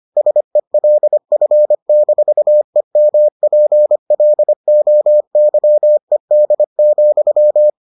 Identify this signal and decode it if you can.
Morse code, heard as a keyed tone, about 25 wpm, 600 Hz. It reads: SELF-EMPLOYED,